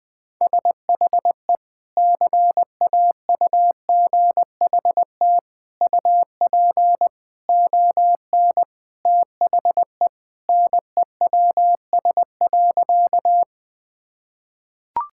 Morse code, heard as a keyed tone, about 21 wpm, 700 Hz. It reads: SHE CAUGHT UP ON THE NEWS.
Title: Sentences with 3-Word Phrases - Spoken Beginning - 20wpm